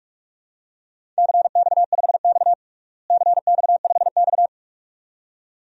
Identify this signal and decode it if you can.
Morse code, heard as a keyed tone, about 41 wpm, 700 Hz. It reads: CX5X CX5X
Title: DXCC-100 Call Signs - 4 Characters - Repeated 2x - 45wpm